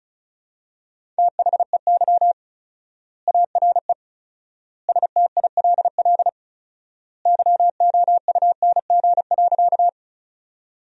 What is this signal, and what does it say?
Morse code, heard as a keyed tone, about 36 wpm, 700 Hz. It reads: THEY ARE STILL YOUNG.
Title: 4x Word-Spacing - Sentences from Top 400 Words - 35wpm